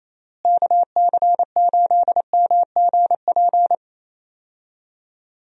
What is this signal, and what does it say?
Morse code, heard as a keyed tone, about 28 wpm, 700 Hz. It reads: KC8MGP
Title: USA Call Signs - 6 Characters - Repeated 2x - 28wpm